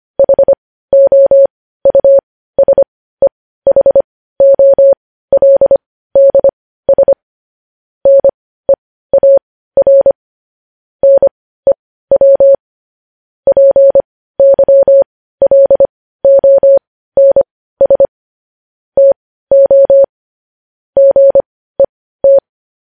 Morse code, heard as a keyed tone, about 25 wpm, spaced as for 15 wpm, 550 Hz. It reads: HOUSEHOLDS NEAR NEW PYLONS TO GET
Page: News headlines in Morse code at 15 wpm